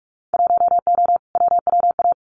Morse code, heard as a keyed tone, about 46 wpm, 700 Hz. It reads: 1O WWA
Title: Winter Field Day Exchanges 45wpm